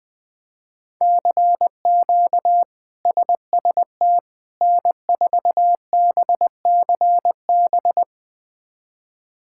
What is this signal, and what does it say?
Morse code, heard as a keyed tone, about 20 wpm, 700 Hz. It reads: CQ SST N4BCB
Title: K1USN SST QSO Practice - 20wpm